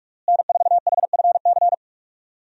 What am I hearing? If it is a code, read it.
Morse code, heard as a keyed tone, about 41 wpm, 700 Hz. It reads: N4HFC
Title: USA Call Signs - 5 Characters - Repeated 3x - 45wpm